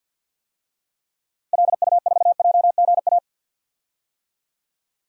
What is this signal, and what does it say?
Morse code, heard as a keyed tone, about 49 wpm, 700 Hz. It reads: LU4JGA
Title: DXCC-100 Call Signs - 6 Characters - Repeated 3x - 50wpm